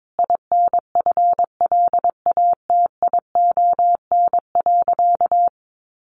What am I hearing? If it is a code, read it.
Morse code, heard as a keyed tone, about 22 wpm, 700 Hz. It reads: INFLATION.
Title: Sentences from Top 1100 Words 22wpm